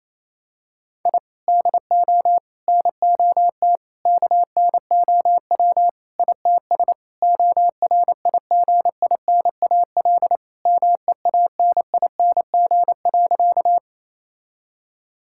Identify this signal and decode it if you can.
Morse code, heard as a keyed tone, about 28 wpm, 700 Hz. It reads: I DO NOT KNOW ITS ORIGINAL MEANING.
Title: Sentences from Top 1000 Words - Review Part 6 - 28wpm